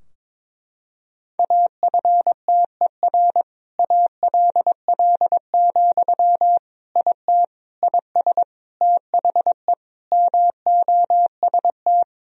Morse code, heard as a keyed tone, about 22 wpm, 700 Hz. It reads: AFTER ALL, IT IS THE MOST
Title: Sentences from Top 1000 Words - Review Part 4 - 22wpm